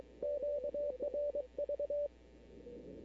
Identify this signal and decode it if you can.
Morse code, heard as a keyed tone, about 23 wpm, 580 Hz. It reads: QRV
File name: CW / 2005-k5tr arrl cw dx -14:00